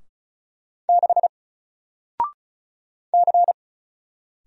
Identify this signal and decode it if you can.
Morse code, heard as a keyed tone, about 36 wpm, 700 Hz. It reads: 6 C
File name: Single-Letter-Number-35wpm